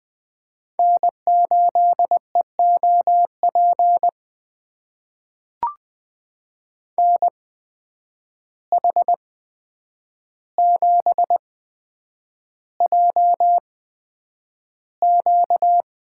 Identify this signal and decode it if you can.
Morse code, heard as a keyed tone, about 20 wpm, spaced as for 9 wpm, 700 Hz. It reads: N8EOP NH7JQ
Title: USA Call Signs - 5 Characters - 8x Character Spacing - 20wpm